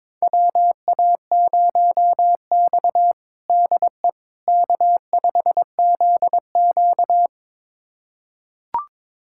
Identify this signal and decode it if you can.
Morse code, heard as a keyed tone, about 22 wpm, 700 Hz. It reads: WA0X DE K5ZQ
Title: USA Callsign DE Callsign - 4 Characters - 22wpm